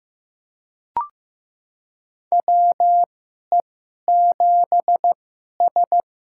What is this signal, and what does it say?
Morse code, heard as a keyed tone, about 15 wpm, spaced as for 11 wpm, 700 Hz. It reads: WE7S
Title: USA Call Signs - 4 Characters - 2x Character Spacing - 15wpm